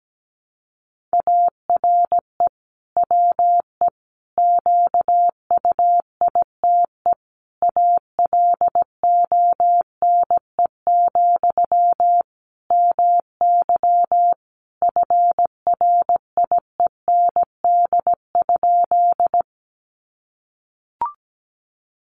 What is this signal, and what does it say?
Morse code, heard as a keyed tone, about 17 wpm, 700 Hz. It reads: ARE WE QUITE ALONE, MY FRIEND?
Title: Sentences from Top 1000 Words 17wpm